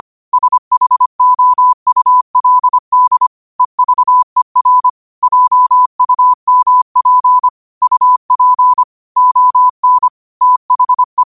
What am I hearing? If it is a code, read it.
Morse code, heard as a keyed tone, about 25 wpm, 1000 Hz. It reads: SHOULD EVER JUMP UP ON THE
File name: qotdfiles / 25wpm